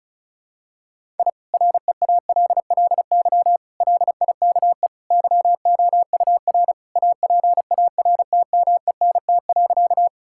Morse code, heard as a keyed tone, about 36 wpm, 700 Hz. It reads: I REALLY LIKE YOUR APARTMENT.
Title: Sentences from Top 1000 Words - Review Part 6 - 35wpm